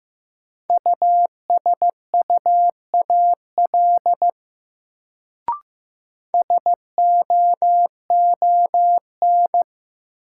Morse code, heard as a keyed tone, about 16 wpm, 700 Hz. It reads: USUAL SOON
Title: Sentences from Top 400 Words - New Word Drill - 15wpm